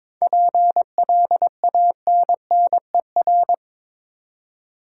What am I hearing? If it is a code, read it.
Morse code, heard as a keyed tone, about 22 wpm, 700 Hz. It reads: PLANNER
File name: Difficult Words with Hints - 22wpm